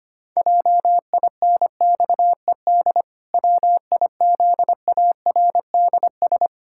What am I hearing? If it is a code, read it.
Morse code, heard as a keyed tone, about 25 wpm, 700 Hz. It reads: JINXED WIZARDS
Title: Pangram Challenge v2 - 25wpm